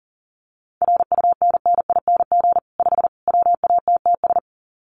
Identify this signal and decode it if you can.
Morse code, heard as a keyed tone, about 40 wpm, 700 Hz. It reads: RUNNING 5 WATTS
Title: Common QSO Phrases 40wpm